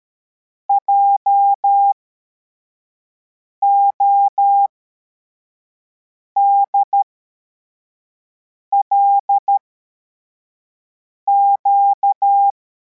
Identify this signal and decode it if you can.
Morse code, heard as a keyed tone, about 13 wpm, spaced as for 4 wpm, 800 Hz. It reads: JODLQ